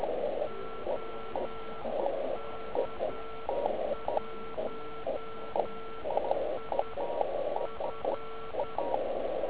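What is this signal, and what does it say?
Morse code, heard as a keyed tone, about 10 wpm, spaced as for 8 wpm, 1340 Hz. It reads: OK1IF
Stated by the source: Slow (QRS) CW test by beginner OK1IFW during beginner Morse skeds on 3559.5 kHz (80 m).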